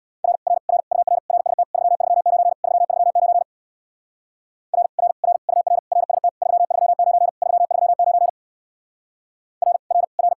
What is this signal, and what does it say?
Morse code, heard as a keyed tone, about 62 wpm, 700 Hz. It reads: R R R UR RST 339 339 R R R UR RST 339 339 R R R
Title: Speed-Racing Common QSO Phrases - 50wpm